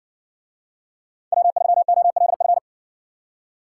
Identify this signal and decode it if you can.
Morse code, heard as a keyed tone, about 60 wpm, 700 Hz. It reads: W3OLF